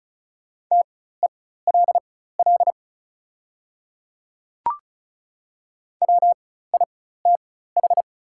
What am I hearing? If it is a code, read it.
Morse code, heard as a keyed tone, about 36 wpm, spaced as for 16 wpm, 700 Hz. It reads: TELL WITH
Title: 4x Character Spacing - Top 200 Words - 35wpm